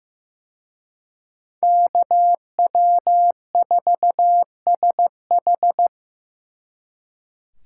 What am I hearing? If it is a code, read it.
Morse code, heard as a keyed tone, about 18 wpm, 700 Hz. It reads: KW4SH
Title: Speed Racing - USA 2x2 Call Signs 15wpm